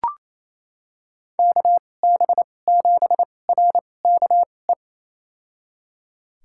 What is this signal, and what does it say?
Morse code, heard as a keyed tone, about 28 wpm, spaced as for 20 wpm, 700 Hz. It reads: KB7RKE
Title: USA Call Signs - 2x Character Spacing - 28wpm